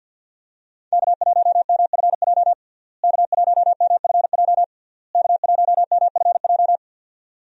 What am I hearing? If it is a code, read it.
Morse code, heard as a keyed tone, about 50 wpm, 700 Hz. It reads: K1MFJ K1MFJ K1MFJ
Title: USA Call Signs - 5 Characters - Repeated 3x - 50wpm